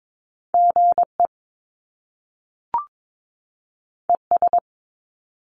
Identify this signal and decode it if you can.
Morse code, heard as a keyed tone, about 22 wpm, 700 Hz. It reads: GE ES
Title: Top Two Letter Combinations in Words 22wpm